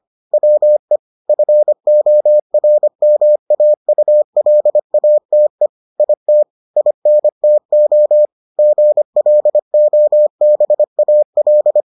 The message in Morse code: WE FORMAULATE IT INTO GLOBAL

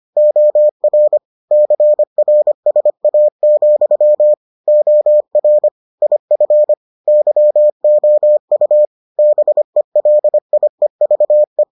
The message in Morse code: OR CRSA, OR IF YOU BELIEVE

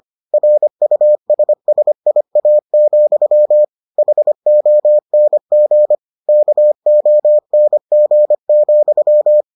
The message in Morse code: RUSSIA, HONG KONG,

